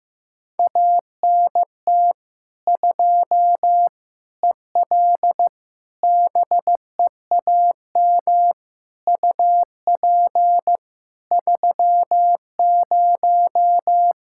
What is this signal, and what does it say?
Morse code, heard as a keyed tone, about 15 wpm, 700 Hz. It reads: ANT 2 EL BEAM UP 30
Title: K1USN SST QSO Practice 15wpm